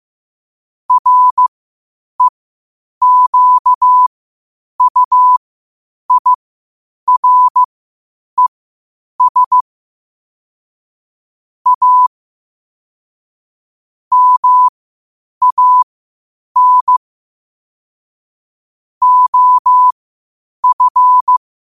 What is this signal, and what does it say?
Morse code, heard as a keyed tone, about 15 wpm, spaced as for 8 wpm, 1000 Hz. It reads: REQUIRES A MAN OF
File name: qotdfiles / 8wpm